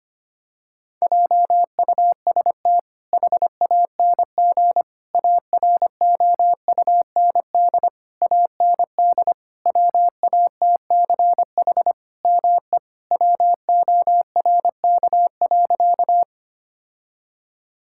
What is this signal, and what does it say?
Morse code, heard as a keyed tone, about 25 wpm, 700 Hz. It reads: JUST HANG AROUND AND WATCH ME WORK.
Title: Sentences from Top 1600 Words 25wpm